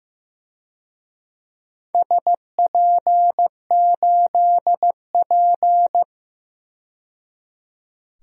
Morse code, heard as a keyed tone, about 15 wpm, 700 Hz. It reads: SP8P
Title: DXCC-100 Call Signs - 4 Characters - 15wpm